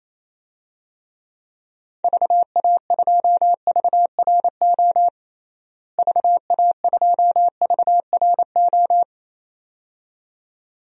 Morse code, heard as a keyed tone, about 27 wpm, 700 Hz. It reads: VA2VRO VA2VRO
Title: DXCC-100 Call Signs - 6 Characters - Repeated 2x - 28wpm